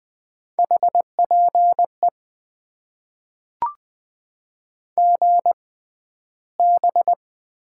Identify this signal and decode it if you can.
Morse code, heard as a keyed tone, about 20 wpm, spaced as for 12 wpm, 700 Hz. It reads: HPE GB
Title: Top 90 QSO Elements - 6x Character Spacing - 20wpm